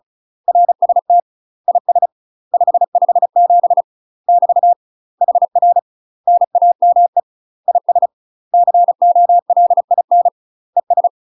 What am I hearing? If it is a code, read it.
Morse code, heard as a keyed tone, about 34 wpm, 700 Hz. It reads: RST IS 557 <BT> HR NAME IS COLIN ES
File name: UK MASTER QSOs TRAINING / 35WPM / UK MASTER QSOs 0005